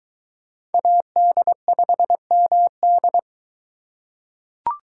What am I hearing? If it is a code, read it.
Morse code, heard as a keyed tone, about 23 wpm, 700 Hz. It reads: AD5MD